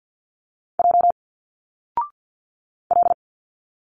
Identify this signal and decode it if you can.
Morse code, heard as a keyed tone, about 49 wpm, 700 Hz. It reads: J L
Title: Single Letters 50wpm